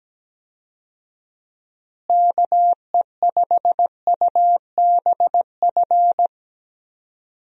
Morse code, heard as a keyed tone, about 17 wpm, 700 Hz. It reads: KE5UBF
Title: USA Call Signs - 6 Characters - Repeated 2x - 17wpm